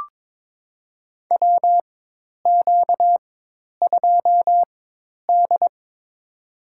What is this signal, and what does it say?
Morse code, heard as a keyed tone, about 22 wpm, spaced as for 10 wpm, 700 Hz. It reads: WQ2D
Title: USA Call Signs - 4 Characters - 4x Character Spacing - 22wpm